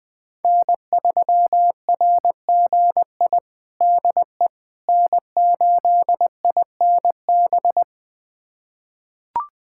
Morse code, heard as a keyed tone, about 21 wpm, 700 Hz. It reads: N3RGI DE N8INB
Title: USA Callsign DE Callsign - 5 Characters - 20wpm